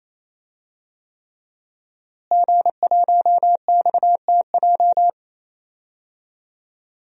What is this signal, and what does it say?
Morse code, heard as a keyed tone, about 28 wpm, 700 Hz. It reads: G1XTJ
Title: DXCC-100 Call Signs - 5 Characters - Repeated 3x - 28wpm